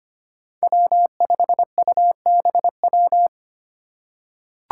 Morse code, heard as a keyed tone, about 25 wpm, 700 Hz. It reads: W5UBW
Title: Five Challenging Letter-Number Combinations 25wpm